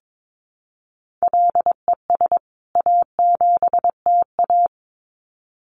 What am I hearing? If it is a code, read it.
Morse code, heard as a keyed tone, about 22 wpm, 700 Hz. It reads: LES A7TA
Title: CWT Exchange 22wpm